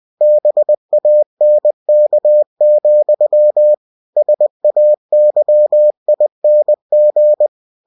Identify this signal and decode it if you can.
Morse code, heard as a keyed tone, about 20 wpm, 600 Hz. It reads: BANK, SAYING
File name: winnie-20wpm